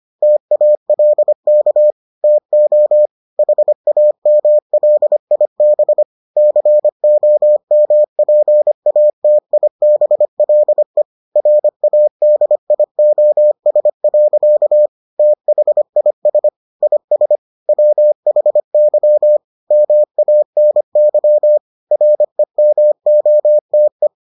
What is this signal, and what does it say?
Morse code, heard as a keyed tone, about 25 wpm, 600 Hz. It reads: TALK TO HAMLIB COMPATIBLE RADIOS. THIS IS WHY MANY REMOTE